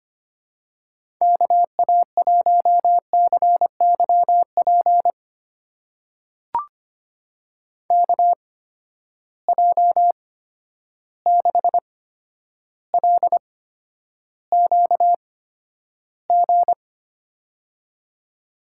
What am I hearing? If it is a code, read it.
Morse code, heard as a keyed tone, about 25 wpm, spaced as for 11 wpm, 700 Hz. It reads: KA1CYP KJ6LQG
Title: USA Call Signs - 6 Characters - 8x Character Spacing - 25wpm